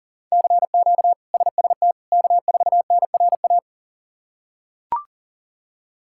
Morse code, heard as a keyed tone, about 40 wpm, 700 Hz. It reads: CQ SST K4NRA
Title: K1USN SST QSO Practice - 40wpm